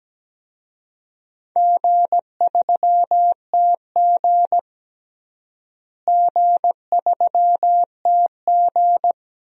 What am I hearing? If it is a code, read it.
Morse code, heard as a keyed tone, about 17 wpm, 700 Hz. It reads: G3TG G3TG
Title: DXCC-100 Call Signs - 4 Characters - Repeated 2x - 17wpm